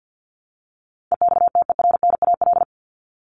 Morse code, heard as a keyed tone, about 50 wpm, 700 Hz. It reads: EXTERNAL
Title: Top 1751-2000 Words 50wpm